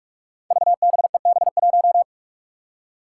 Morse code, heard as a keyed tone, about 46 wpm, 700 Hz. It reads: VBEB1
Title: Five Challenging Letter-Number Combinations 45wpm